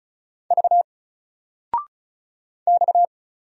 Morse code, heard as a keyed tone, about 34 wpm, 700 Hz. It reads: V X
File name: Single-Letter-Number-35wpm